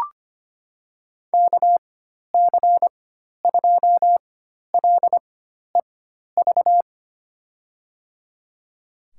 Morse code, heard as a keyed tone, about 25 wpm, spaced as for 12 wpm, 700 Hz. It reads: KC2LEV
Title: USA Call Signs - 4x Character Spacing - 25wpm